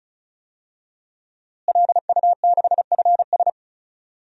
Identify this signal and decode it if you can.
Morse code, heard as a keyed tone, about 36 wpm, 700 Hz. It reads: LU6FS
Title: DXCC-100 Call Signs - 5 Characters - Repeated 2x - 35wpm